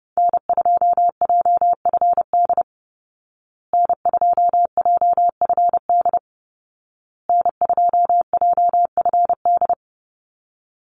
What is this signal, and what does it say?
Morse code, heard as a keyed tone, about 30 wpm, 700 Hz. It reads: N2JFD N2JFD N2JFD
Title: Call Signs Repeated 2x - 30wpm